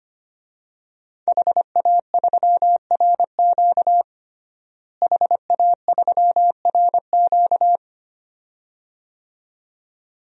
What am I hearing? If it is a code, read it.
Morse code, heard as a keyed tone, about 25 wpm, 700 Hz. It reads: HA3RQ HA3RQ
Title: DXCC-100 Call Signs - 5 Characters - Repeated 2x - 25wpm